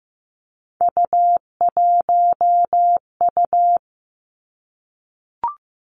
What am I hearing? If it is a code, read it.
Morse code, heard as a keyed tone, about 15 wpm, 700 Hz. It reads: U1U
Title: Three Challenging Letter-Number Combinations 15wpm